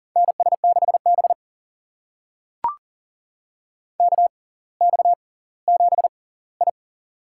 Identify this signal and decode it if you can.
Morse code, heard as a keyed tone, about 38 wpm, spaced as for 20 wpm, 700 Hz. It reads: NS6B KX7I
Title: USA Call Signs - 4 Characters - 6x Character Spacing - 40wpm